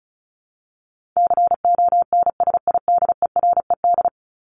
Morse code, heard as a keyed tone, about 34 wpm, 700 Hz. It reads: CONSIDERED
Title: ED Suffix - Top 100 Words with Base Words 35wpm